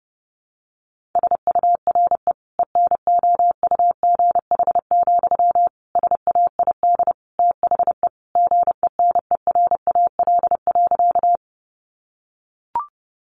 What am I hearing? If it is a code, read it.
Morse code, heard as a keyed tone, about 30 wpm, 700 Hz. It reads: SURE ENOUGH, SAID THE GENERAL.
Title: Sentences from Top 600 Words 30wpm